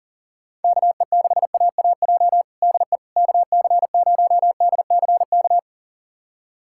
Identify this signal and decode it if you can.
Morse code, heard as a keyed tone, about 40 wpm, 700 Hz. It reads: KE6AAJ DE KC0DCK
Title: USA Callsign DE Callsign - 6 Characters - 40wpm